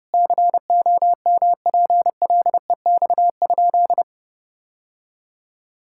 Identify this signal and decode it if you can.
Morse code, heard as a keyed tone, about 29 wpm, 700 Hz. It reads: COMPLEX?
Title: Sentences from Top 1000 Words - Review Part 6 - 30wpm